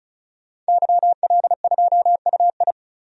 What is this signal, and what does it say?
Morse code, heard as a keyed tone, about 36 wpm, 700 Hz. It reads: YL2UI
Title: DXCC-100 Call Signs - 5 Characters - 35wpm